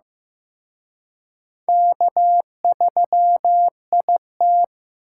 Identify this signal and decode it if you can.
Morse code, heard as a keyed tone, about 18 wpm, 700 Hz. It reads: K3IT
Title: Speed Racing - USA 1x2 Call Signs 15wpm